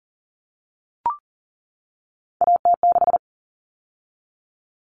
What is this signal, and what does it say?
Morse code, heard as a keyed tone, about 40 wpm, 700 Hz. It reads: AT6
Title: All Cut Numbers - 3-Digit 40wpm